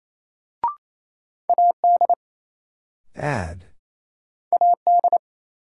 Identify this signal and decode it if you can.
Morse code, heard as a keyed tone, about 28 wpm, 700 Hz. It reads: AD AD
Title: Sentences from Top 1300 Words - New Word Drill Alt - 28wpm